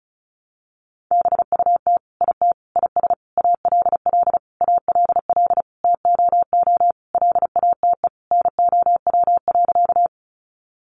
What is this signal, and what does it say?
Morse code, heard as a keyed tone, about 34 wpm, 700 Hz. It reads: BUT IT IS ALL ALL TOO LATE NOW.
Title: Sentences from Top 300 Words 35wpm